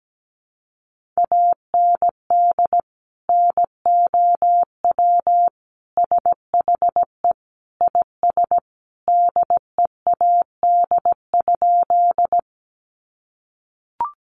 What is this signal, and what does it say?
Morse code, heard as a keyed tone, about 17 wpm, 700 Hz. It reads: AND NOW SHE IS DEAD?
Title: Sentences from Top 1500 Words 17wpm